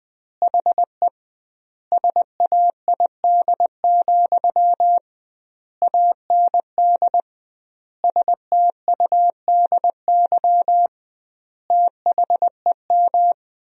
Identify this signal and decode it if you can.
Morse code, heard as a keyed tone, about 20 wpm, 700 Hz. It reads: HE SAID, AND STUDY THEM
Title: 2x Word-Spacing - Sentences from Top 300 Words - 20wpm